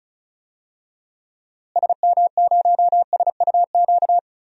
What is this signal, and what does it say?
Morse code, heard as a keyed tone, about 36 wpm, 700 Hz. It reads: SM0SUQ
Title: DXCC-100 Call Signs - 6 Characters - Repeated 3x - 35wpm